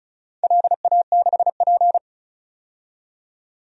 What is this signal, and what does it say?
Morse code, heard as a keyed tone, about 34 wpm, 700 Hz. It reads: LA6P
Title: DXCC-100 Call Signs - 4 Characters - 35wpm